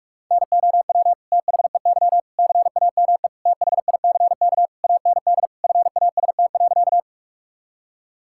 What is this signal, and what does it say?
Morse code, heard as a keyed tone, about 41 wpm, 700 Hz. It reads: NOW THEY CAME THICK AND FAST.
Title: Sentences from Top 1600 Words 45wpm